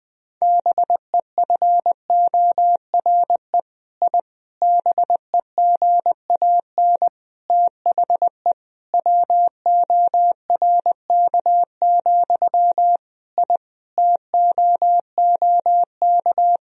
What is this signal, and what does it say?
Morse code, heard as a keyed tone, about 20 wpm, 700 Hz. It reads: BEFORE I BEGAN THE WORK, I TOOK
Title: Sentences from Top 1000 Words - Review Part 4 - 20wpm